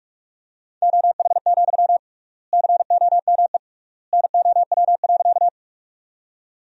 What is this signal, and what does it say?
Morse code, heard as a keyed tone, about 45 wpm, 700 Hz. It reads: OH, COME NOW.
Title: 3x Word-Spacing - Sentences from Top 500 Words - 45wpm